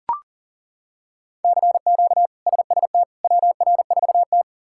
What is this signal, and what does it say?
Morse code, heard as a keyed tone, about 40 wpm, 700 Hz. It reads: CQ SST WR4T
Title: K1USN SST QSO Practice - 40wpm